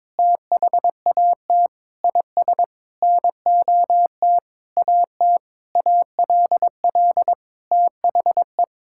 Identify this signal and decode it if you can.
Morse code, heard as a keyed tone, about 22 wpm, 700 Hz. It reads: THAT IS NOT AT ALL THE
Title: Sentences from Top 1000 Words - Review Part 6 - 22wpm